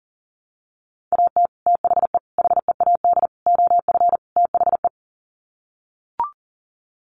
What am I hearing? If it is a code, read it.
Morse code, heard as a keyed tone, about 40 wpm, 700 Hz. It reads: AT THE HEAD OF THE
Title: Sets of 5 Words 40wpm